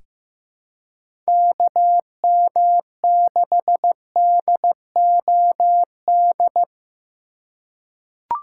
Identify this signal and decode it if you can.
Morse code, heard as a keyed tone, about 15 wpm, 700 Hz. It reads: KM6DOD
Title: USA Call Signs - 6 Characters - 2x Character Spacing - 15wpm